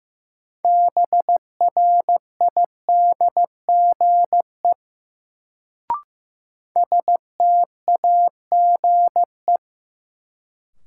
Morse code, heard as a keyed tone, about 15 wpm, 700 Hz. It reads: BRIDGE STAGE
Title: Sentences from Top 1200 Words - New Word Drill - 15wpm